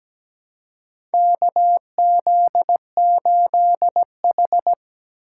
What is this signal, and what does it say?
Morse code, heard as a keyed tone, about 17 wpm, 700 Hz. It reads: KZ8H
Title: USA Call Signs - 4 Characters - 3x Character Spacing - 17wpm